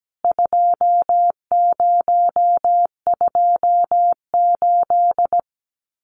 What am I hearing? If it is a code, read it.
Morse code, heard as a keyed tone, about 17 wpm, 700 Hz. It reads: 2028
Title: CWT Exchange 17wpm